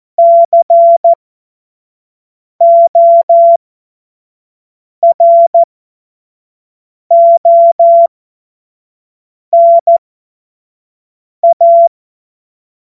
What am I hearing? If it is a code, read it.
Morse code, heard as a keyed tone, about 14 wpm, spaced as for 5 wpm, 680 Hz. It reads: CORONA
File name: Morsekurs / Witze / Witz 06. Juli 2021, 25 F70